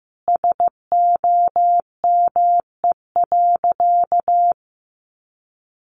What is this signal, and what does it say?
Morse code, heard as a keyed tone, about 15 wpm, 700 Hz. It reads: SOME.
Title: Sentences from Top 800 Words 15wpm